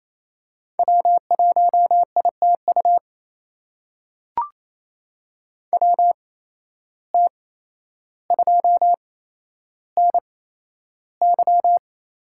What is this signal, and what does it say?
Morse code, heard as a keyed tone, about 28 wpm, spaced as for 12 wpm, 700 Hz. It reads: W1ITU WT2NY
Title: USA Call Signs - 5 Characters - 8x Character Spacing - 28wpm